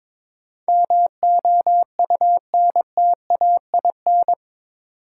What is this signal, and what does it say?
Morse code, heard as a keyed tone, about 22 wpm, 700 Hz. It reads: MOUNTAIN
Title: Sentences from Top 1300 Words - New Word Drill Alt - 22wpm